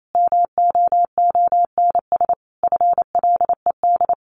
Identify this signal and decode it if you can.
Morse code, heard as a keyed tone, about 28 wpm, 700 Hz. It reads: MOONS FLED
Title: A Sound Of Thunder 28wpm